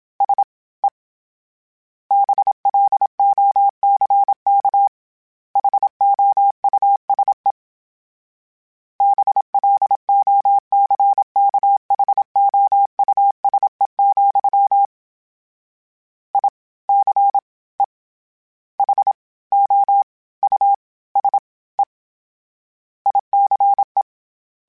Morse code, heard as a keyed tone, about 27 wpm, 800 Hz. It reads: S E BLOCK HOUSE BLOCKHOUSE, I C E H O U S E ICE
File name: General / General-mp3s / Houses @25